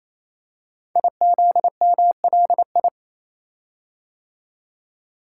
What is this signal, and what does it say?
Morse code, heard as a keyed tone, about 28 wpm, 700 Hz. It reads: IZMLI
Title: Code Groups - 5 Letters - 28wpm